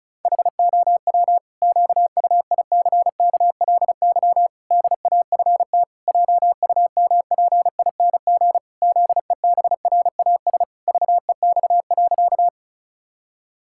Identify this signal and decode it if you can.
Morse code, heard as a keyed tone, about 34 wpm, 700 Hz. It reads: HOW QUICKLY DAFT JUMPING ZEBRAS VEX.
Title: Pangram Challenge v2 - 35wpm